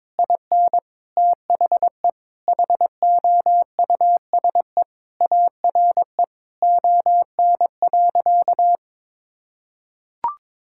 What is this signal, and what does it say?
Morse code, heard as a keyed tone, about 23 wpm, 700 Hz. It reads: IN THE HOUSE ARE ON.
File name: practice / 22wpm / Sentences with 2-Word Phrases - 22wpm